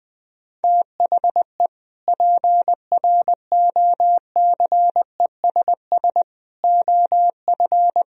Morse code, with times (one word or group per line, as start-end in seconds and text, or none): 0.6-1.7 THE
2.1-6.2 PROCESS
6.6-8.0 OF